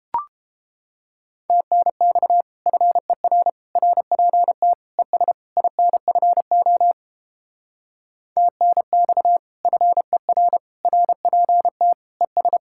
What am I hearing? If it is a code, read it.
Morse code, heard as a keyed tone, about 29 wpm, 700 Hz. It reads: TNX FER RPT ES INFO TNX FER RPT ES